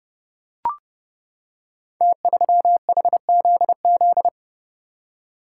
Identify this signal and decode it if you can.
Morse code, heard as a keyed tone, about 30 wpm, 700 Hz. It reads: T3HZZ
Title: Code Groups - 5 Letter:Numbers - 30wpm